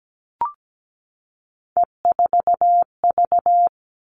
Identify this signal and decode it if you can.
Morse code, heard as a keyed tone, about 17 wpm, 700 Hz. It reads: E4V
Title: All Cut Numbers - 3-Digit 17wpm